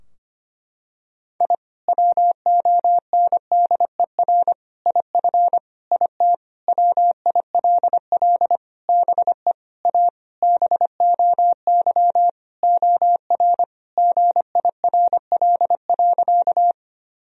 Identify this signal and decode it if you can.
Morse code, heard as a keyed tone, about 25 wpm, 700 Hz. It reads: I WONDER IF IT WILL BE A BOY OR GIRL.
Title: Sentences from Top 1000 Words - Review Part 3 - 25wpm